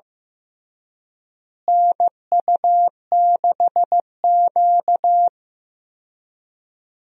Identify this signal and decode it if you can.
Morse code, heard as a keyed tone, about 15 wpm, 700 Hz. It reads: NU6Q
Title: USA Call Signs - 4 Characters - Repeated 3x - 15wpm